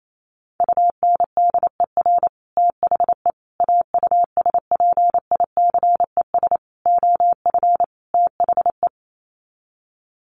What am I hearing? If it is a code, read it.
Morse code, heard as a keyed tone, about 28 wpm, 700 Hz. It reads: UNDER THE AUSPICES OF THE
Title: Sets of 5 Words Encore2 28wpm